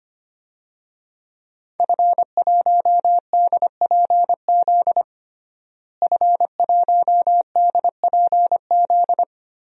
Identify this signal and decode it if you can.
Morse code, heard as a keyed tone, about 25 wpm, 700 Hz. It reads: F1DPZ F1DPZ
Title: DXCC-100 Call Signs - 5 Characters - Repeated 3x - 25wpm